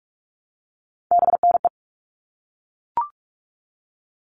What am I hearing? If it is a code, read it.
Morse code, heard as a keyed tone, about 44 wpm, 700 Hz. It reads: BNE
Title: All Cut Numbers - 3-Digit 45wpm